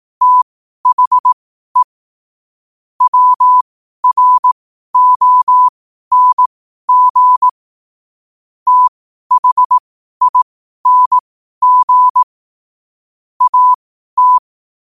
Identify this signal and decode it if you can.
Morse code, heard as a keyed tone, about 18 wpm, spaced as for 12 wpm, 1000 Hz. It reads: THE WRONG THING AT
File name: qotdfiles / 12wpm